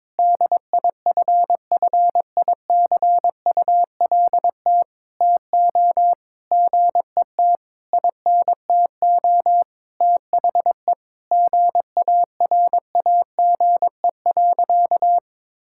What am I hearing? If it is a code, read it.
Morse code, heard as a keyed tone, about 22 wpm, 700 Hz. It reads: DIFFICULT TO GET INTO THE GARAGE.
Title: Sentences from Top 1600 Words 22wpm